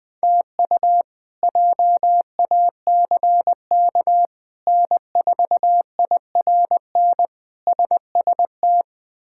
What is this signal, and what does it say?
Morse code, heard as a keyed tone, about 20 wpm, 700 Hz. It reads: TU JACK N4IRN SST
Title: K1USN SST QSO Practice - 20wpm